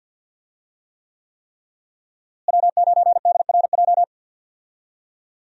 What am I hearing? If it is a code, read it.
Morse code, heard as a keyed tone, about 50 wpm, 700 Hz. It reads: W9DRJ
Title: USA Call Signs - 5 Characters - Repeated 3x - 50wpm